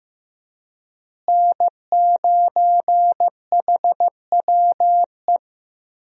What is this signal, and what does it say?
Morse code, heard as a keyed tone, about 15 wpm, 700 Hz. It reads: N9HWE
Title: USA Call Signs - 5 Characters - Repeated 2x - 15wpm